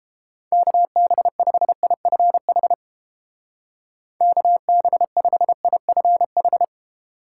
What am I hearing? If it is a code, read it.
Morse code, heard as a keyed tone, about 31 wpm, 700 Hz. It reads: KB5IFH KB5IFH